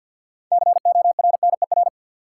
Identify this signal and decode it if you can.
Morse code, heard as a keyed tone, about 53 wpm, 700 Hz. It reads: CORNER
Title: Six Letter Words - 50wpm